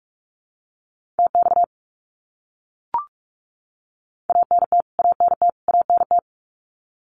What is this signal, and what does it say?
Morse code, heard as a keyed tone, about 44 wpm, 700 Hz. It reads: TX ANT ANT ANT
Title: Abbreviations Repeated 3x - 45wpm